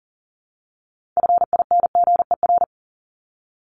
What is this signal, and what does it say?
Morse code, heard as a keyed tone, about 40 wpm, 700 Hz. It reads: FINGER
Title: Top 1501-1750 Words 40wpm